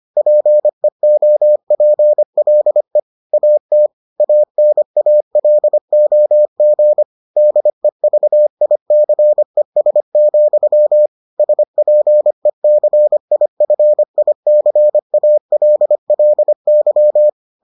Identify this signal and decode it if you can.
Morse code, heard as a keyed tone, about 25 wpm, 600 Hz. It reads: PEOPLE AT ANALOG DEVICES, SPECIFICALLY